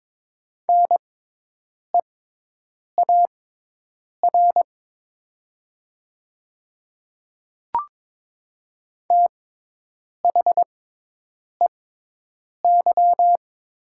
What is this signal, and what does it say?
Morse code, heard as a keyed tone, about 22 wpm, spaced as for 8 wpm, 700 Hz. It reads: NEAR THEY
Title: 6x Character Spacing - Top 200 Words - 22wpm